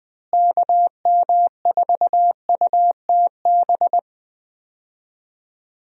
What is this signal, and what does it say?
Morse code, heard as a keyed tone, about 20 wpm, 700 Hz. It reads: KM4UTB
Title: USA Call Signs - 6 Characters - Repeated 2x - 20wpm